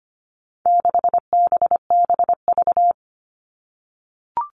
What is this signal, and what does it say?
Morse code, heard as a keyed tone, about 25 wpm, 700 Hz. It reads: 6BBV
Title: All Cut Numbers - 4-Digit 25wpm